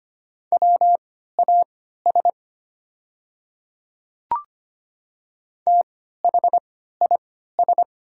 Morse code, heard as a keyed tone, about 25 wpm, spaced as for 14 wpm, 700 Hz. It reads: WAS THIS
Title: 3x Character Spacing - Top 300 Words - 25wpm